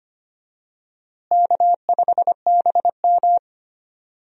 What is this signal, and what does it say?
Morse code, heard as a keyed tone, about 25 wpm, 700 Hz. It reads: K5BM
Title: USA Call Signs - 4 Characters - Repeated 3x - 25wpm